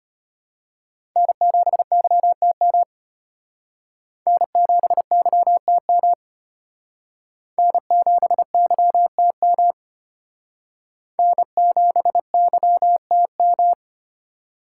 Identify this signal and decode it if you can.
Morse code, heard as a keyed tone, about 34 wpm, 700 Hz. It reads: N7YTM N7YTM N7YTM N7YTM